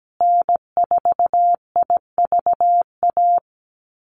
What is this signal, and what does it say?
Morse code, heard as a keyed tone, about 17 wpm, 700 Hz. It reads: N4IVA
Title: USA Call Signs Repeated 2x 17wpm